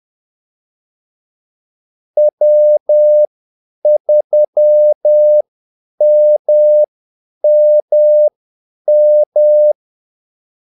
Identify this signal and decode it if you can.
Morse code, heard as a keyed tone, about 10 wpm, spaced as for 8 wpm, 600 Hz. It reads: W3MMM
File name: Callsigns-10-8wpm